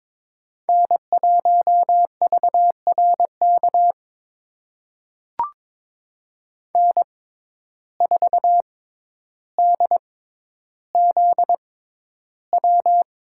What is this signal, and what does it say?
Morse code, heard as a keyed tone, about 22 wpm, spaced as for 11 wpm, 700 Hz. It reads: N1VRK N4DZW
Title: USA Call Signs - 5 Characters - 6x Character Spacing - 22wpm